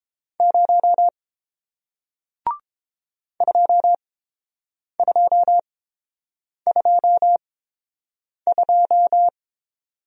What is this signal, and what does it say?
Morse code, heard as a keyed tone, about 30 wpm, spaced as for 7 wpm, 700 Hz. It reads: 0 2222